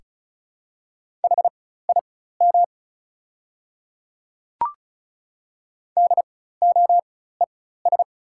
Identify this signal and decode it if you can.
Morse code, heard as a keyed tone, about 36 wpm, spaced as for 16 wpm, 700 Hz. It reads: HIM DOES
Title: 4x Character Spacing - Top 200 Words - 35wpm